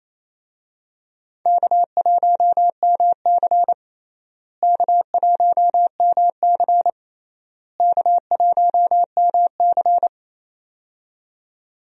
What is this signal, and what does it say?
Morse code, heard as a keyed tone, about 28 wpm, 700 Hz. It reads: K1MC K1MC K1MC
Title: USA Call Signs - 4 Characters - Repeated 3x - 28wpm